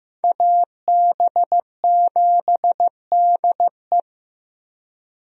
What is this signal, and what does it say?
Morse code, heard as a keyed tone, about 15 wpm, 700 Hz. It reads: AB7DE
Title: USA Call Signs - 5 Characters - Repeated 3x - 15wpm